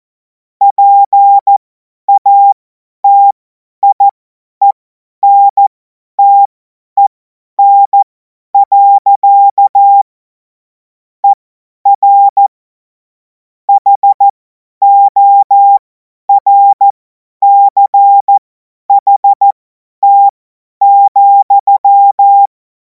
Morse code, heard as a keyed tone, about 14 wpm, spaced as for 10 wpm, 800 Hz. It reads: PATIENTEN. ER HORCHT,
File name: Morsekurs / Witze / Witz 18. August 2026 50 F70